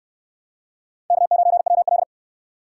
Morse code, heard as a keyed tone, about 61 wpm, 700 Hz. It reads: K9WL